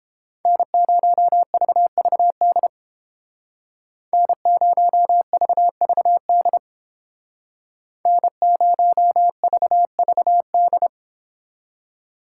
Morse code, heard as a keyed tone, about 29 wpm, 700 Hz. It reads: N0VVD N0VVD N0VVD